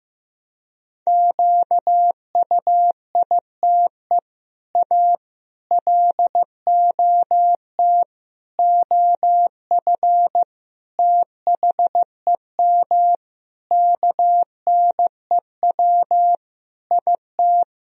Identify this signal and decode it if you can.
Morse code, heard as a keyed tone, about 15 wpm, 700 Hz. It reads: QUITE A LOT OF THEM KNEW IT
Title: Sentences from Top 1000 Words - Review Part 3 - 15wpm